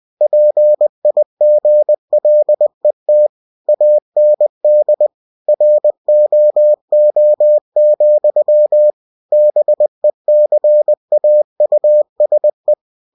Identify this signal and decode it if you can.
Morse code, heard as a keyed tone, about 20 wpm, 600 Hz. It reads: PIGLET AND ROO, BECAUSE
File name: cw-books / winnie-the-pooh / winnie-20wpm-page-0107